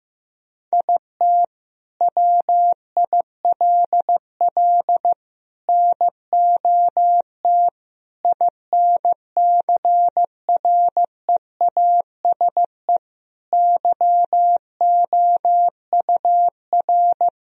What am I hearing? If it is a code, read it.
Morse code, heard as a keyed tone, about 15 wpm, 700 Hz. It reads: IT WILL NOT INCREASE YOUR
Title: Sentences from Top 1000 Words - Review Part 4 - 15wpm